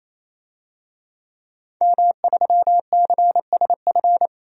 Morse code, heard as a keyed tone, about 28 wpm, 700 Hz. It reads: M3CSF
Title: DXCC-100 Call Signs - 5 Characters - 28wpm